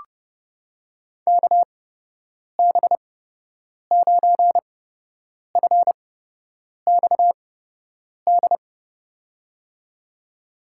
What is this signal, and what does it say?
Morse code, heard as a keyed tone, about 32 wpm, spaced as for 8 wpm, 700 Hz. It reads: KB9FXD
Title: USA Call Signs - 6 Characters - 8x Character Spacing - 30wpm